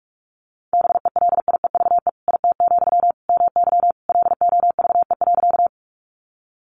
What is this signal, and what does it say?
Morse code, heard as a keyed tone, about 44 wpm, 700 Hz. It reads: BELIEVE IT, MY LOVE.
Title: Sentences from Top 1000 Words 45wpm